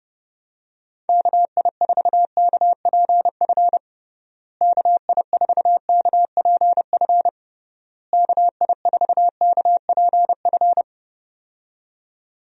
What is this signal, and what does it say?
Morse code, heard as a keyed tone, about 30 wpm, 700 Hz. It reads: KI4KPF KI4KPF KI4KPF
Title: USA Call Signs - 6 Characters - Repeated 3x - 30wpm